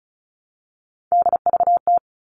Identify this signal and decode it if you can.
Morse code, heard as a keyed tone, about 34 wpm, 700 Hz. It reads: DVT
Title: All Cut Numbers - 3-Digit 35wpm